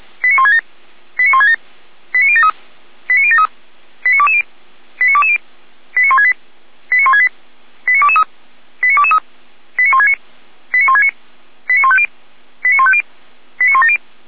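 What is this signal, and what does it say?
Five-tone selective call (ZVEI), heard as quick runs of five tones, tones of about 70 ms. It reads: 89167 89167 89073 89073 89220 89220 89168 89168 89223 89223 89169 89169 89150 89150 89170
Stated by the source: Received by SDR, logged on a server of Feuerwehr Orlamünde (fire brigade).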